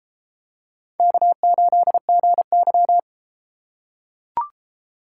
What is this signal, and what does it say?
Morse code, heard as a keyed tone, about 33 wpm, 700 Hz. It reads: K8GY